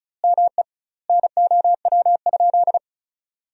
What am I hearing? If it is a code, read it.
Morse code, heard as a keyed tone, about 34 wpm, 700 Hz. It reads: ME NOW?
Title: 2x Word-Spacing - Sentences from Top 400 Words - 35wpm